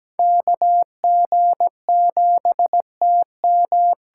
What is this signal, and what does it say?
Morse code, heard as a keyed tone, about 17 wpm, 700 Hz. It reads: KG7TM
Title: USA Call Signs - 8x Character Spacing - 17wpm